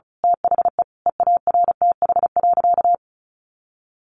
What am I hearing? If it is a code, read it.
Morse code, heard as a keyed tone, about 34 wpm, 700 Hz. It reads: THE EARTH.
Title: A Sound Of Thunder 35wpm